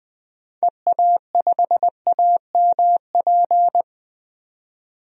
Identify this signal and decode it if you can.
Morse code, heard as a keyed tone, about 20 wpm, 700 Hz. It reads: EA5AMP
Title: DXCC-100 Call Signs - 6 Characters - 20wpm